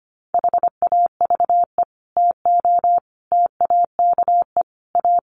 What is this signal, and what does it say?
Morse code, heard as a keyed tone, about 25 wpm, 700 Hz. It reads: HAVE TO TAKE A
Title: Sets of 5 Words Encore 25wpm